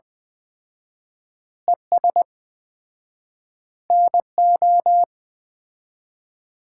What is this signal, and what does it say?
Morse code, heard as a keyed tone, about 20 wpm, 700 Hz. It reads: ES NO
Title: Common QSO Phrases - 4x Word-Spacing - 20wpm